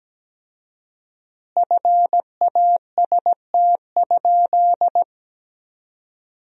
Morse code, heard as a keyed tone, about 17 wpm, 700 Hz. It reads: FAST?
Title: 4x Word-Spacing - Sentences from Top 500 Words - 17wpm